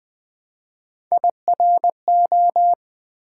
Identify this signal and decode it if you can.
Morse code, heard as a keyed tone, about 20 wpm, 700 Hz. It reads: IRO
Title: Build A Word - 1 Letter At A Time - 4-Letter Words - 20wpm